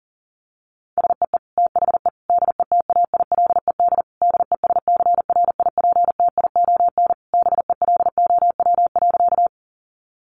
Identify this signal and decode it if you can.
Morse code, heard as a keyed tone, about 40 wpm, 700 Hz. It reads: SEE THE DETAILED DESCRIPTION BELOW.
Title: Sentences from Top 1500 Words 40wpm